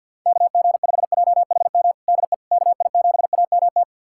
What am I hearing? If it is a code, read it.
Morse code, heard as a keyed tone, about 49 wpm, 700 Hz. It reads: KG5JHM DE KI7AMT
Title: USA Callsign DE Callsign - 6 Characters - 50wpm